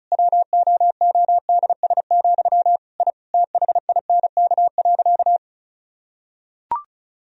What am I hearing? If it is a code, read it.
Morse code, heard as a keyed tone, about 36 wpm, 700 Hz. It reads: WOODS, I THINK.
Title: Sentences from Top 1000 Words - Review Part 2 - 35wpm